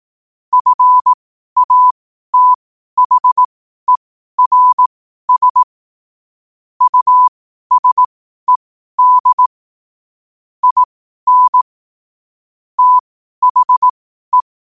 Morse code, heard as a keyed tone, about 18 wpm, spaced as for 12 wpm, 1000 Hz. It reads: FATHERS USED IN THE
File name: qotdfiles / 12wpm